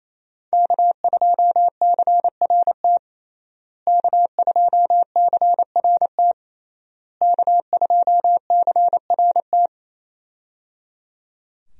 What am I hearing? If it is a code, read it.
Morse code, heard as a keyed tone, about 28 wpm, 700 Hz. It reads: K2CRT K2CRT K2CRT
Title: USA Call Signs - 5 Characters - Repeated 3x - 28wpm